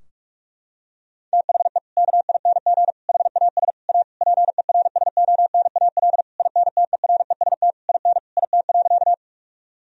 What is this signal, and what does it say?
Morse code, heard as a keyed tone, about 46 wpm, 700 Hz. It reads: THE KING HAS A PERSONAL INTEREST IN IT.
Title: Sentences from Top 1000 Words - Review Part 4 - 45wpm